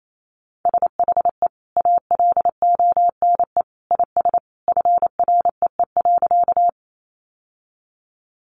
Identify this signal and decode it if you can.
Morse code, heard as a keyed tone, about 28 wpm, 700 Hz. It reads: SHE ALONE IS FREE.
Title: Sentences from Top 1000 Words 28wpm